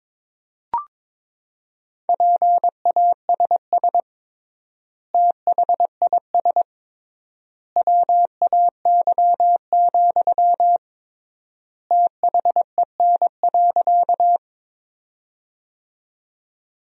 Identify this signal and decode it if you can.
Morse code, heard as a keyed tone, about 22 wpm, 700 Hz. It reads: PASS THIS WAY, THEN.
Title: 3x Word-Spacing - Sentences from Top 400 Words - 22wpm